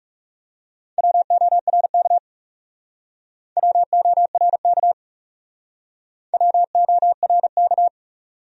Morse code, heard as a keyed tone, about 40 wpm, 700 Hz. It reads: WORK WORK WORK